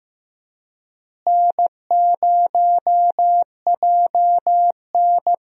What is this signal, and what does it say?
Morse code, heard as a keyed tone, about 15 wpm, 700 Hz. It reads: N0JN
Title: USA Call Signs - 4 Characters - Repeated 2x - 15wpm